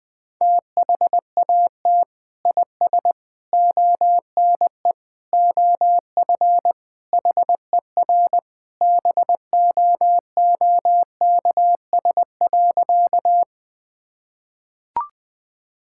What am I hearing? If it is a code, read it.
Morse code, heard as a keyed tone, about 20 wpm, 700 Hz. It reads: THAT IS ONE OF HER BOOKS.
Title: Sentences with 2-Word Phrases - 20wpm